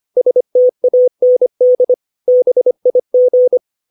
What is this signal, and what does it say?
Morse code, heard as a keyed tone, about 25 wpm, 500 Hz. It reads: STAND BIG
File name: 632 words / 632 Words 25WPM